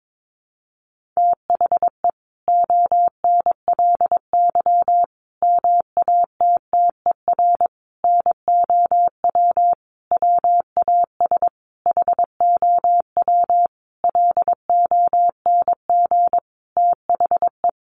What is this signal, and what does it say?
Morse code, heard as a keyed tone, about 22 wpm, 700 Hz. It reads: THE ONLY MATTER NOW WAS HOW LONG THE
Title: Sentences from Top 1400 Words 22wpm